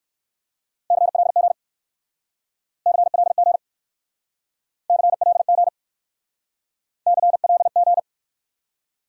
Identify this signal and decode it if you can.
Morse code, heard as a keyed tone, about 58 wpm, 700 Hz. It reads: CLG CLG CLG CLG